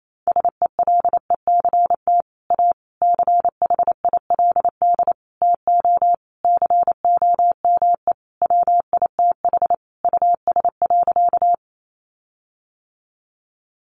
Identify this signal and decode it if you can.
Morse code, heard as a keyed tone, about 28 wpm, 700 Hz. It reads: SELECT A CHILD TO COME WITH US.
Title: Sentences from Top 1300 Words 28wpm